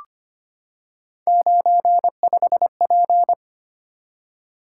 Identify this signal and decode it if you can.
Morse code, heard as a keyed tone, about 25 wpm, 700 Hz. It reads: 95P
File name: Three Letter-Number Combinations - 25wpm